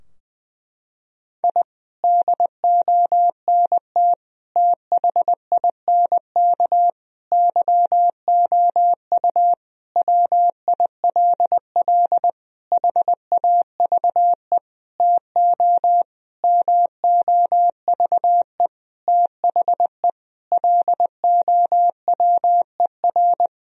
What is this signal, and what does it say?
Morse code, heard as a keyed tone, about 20 wpm, 700 Hz. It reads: I DONT THINK YOU WILL HAVE TO MOVE THE LOWER
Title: Sentences from Top 1000 Words - Review Part 4 - 20wpm